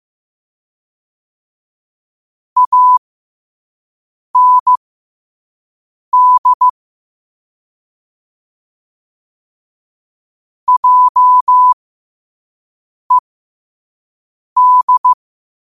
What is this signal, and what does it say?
Morse code, heard as a keyed tone, about 16 wpm, spaced as for 5 wpm, 1000 Hz. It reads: AND JED
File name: qotdfiles / 5wpm